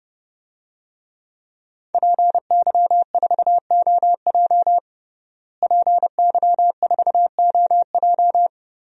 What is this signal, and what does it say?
Morse code, heard as a keyed tone, about 30 wpm, 700 Hz. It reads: PY4OJ PY4OJ
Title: DXCC-100 Call Signs - 5 Characters - Repeated 3x - 30wpm